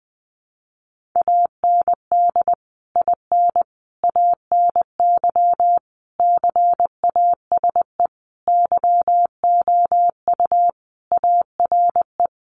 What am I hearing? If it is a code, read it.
Morse code, heard as a keyed tone, about 20 wpm, 700 Hz. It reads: AND IN ANY CASE YOU ARE
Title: Sentences from Top 1200 Words 20wpm